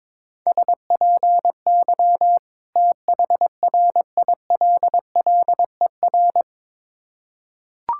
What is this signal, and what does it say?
Morse code, heard as a keyed tone, about 23 wpm, 690 Hz. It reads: SPY THRILLER
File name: Difficult Words with Hints - 22wpm